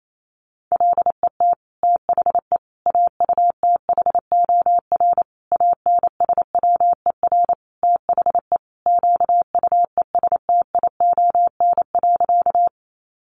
Morse code, heard as a keyed tone, about 28 wpm, 700 Hz. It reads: LET THE AUTHOR ANSWER THE QUESTION.
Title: Sentences from Top 1200 Words 28wpm